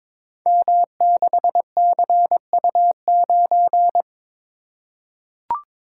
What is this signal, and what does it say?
Morse code, heard as a keyed tone, about 22 wpm, 700 Hz. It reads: M6CU9
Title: Code Groups - 5 Letter:Numbers - 22wpm